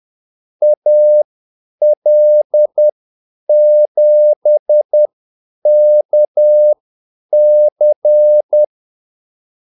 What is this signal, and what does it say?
Morse code, heard as a keyed tone, about 10 wpm, spaced as for 8 wpm, 600 Hz. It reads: AL7KC
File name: Callsigns-10-8wpm